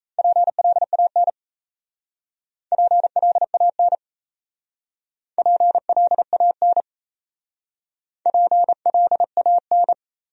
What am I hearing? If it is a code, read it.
Morse code, heard as a keyed tone, about 37 wpm, 700 Hz. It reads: PLAN PLAN PLAN PLAN